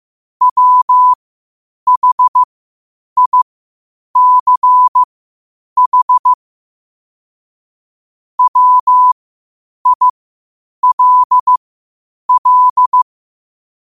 Morse code, heard as a keyed tone, about 15 wpm, spaced as for 8 wpm, 1000 Hz. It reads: WHICH WILL
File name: qotdfiles / 8wpm